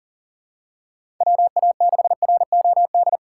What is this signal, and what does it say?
Morse code, heard as a keyed tone, about 40 wpm, 700 Hz. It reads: WA6ROD
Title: USA Call Signs - 6 Characters - 6x Character Spacing - 40wpm